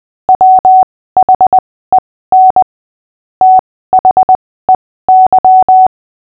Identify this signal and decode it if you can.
Morse code, heard as a keyed tone, about 20 wpm, spaced as for 15 wpm, 750 Hz. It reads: WHEN THEY
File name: cw / fists / Short-Story-5-15wpm